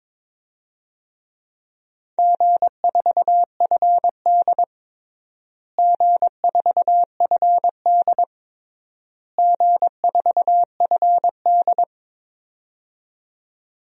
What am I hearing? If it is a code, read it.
Morse code, heard as a keyed tone, about 21 wpm, 700 Hz. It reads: G4FD G4FD G4FD
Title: DXCC-100 Call Signs - 4 Characters - Repeated 3x - 22wpm